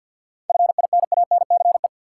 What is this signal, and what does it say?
Morse code, heard as a keyed tone, about 49 wpm, 700 Hz. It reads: FINANCE
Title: Seven Letter Words - 50wpm